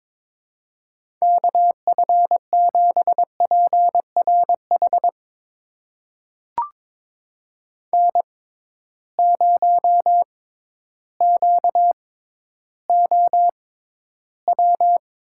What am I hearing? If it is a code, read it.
Morse code, heard as a keyed tone, about 22 wpm, spaced as for 12 wpm, 700 Hz. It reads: KF7PRH N0QOW
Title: USA Call Signs - 6x Character Spacing - 22wpm